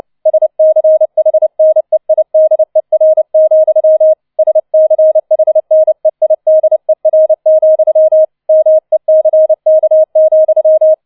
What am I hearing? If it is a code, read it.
Morse code, heard as a keyed tone, about 29 wpm, 630 Hz. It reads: SCHNEIDER, SCHNEIDER, MECK,